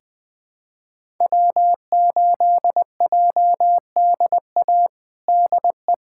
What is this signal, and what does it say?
Morse code, heard as a keyed tone, about 20 wpm, 700 Hz. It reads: W8JDA DE
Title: USA Callsign DE Callsign - 5 Characters - 20wpm